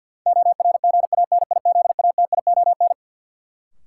Morse code, heard as a keyed tone, about 47 wpm, 700 Hz. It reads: ORGANIZATION
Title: Sentences from Top 1000 Words - Review Part 5 - New Word Drill - 50wpm